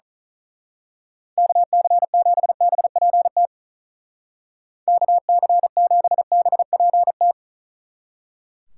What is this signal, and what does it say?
Morse code, heard as a keyed tone, about 39 wpm, 700 Hz. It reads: KC7BPT KC7BPT
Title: Speed Racing - USA Call Signs 35wpm